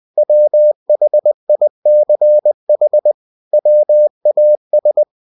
Morse code, heard as a keyed tone, about 20 wpm, 600 Hz. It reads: WHICH WAS
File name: cw-books / winnie-the-pooh / winnie-20wpm-page-0020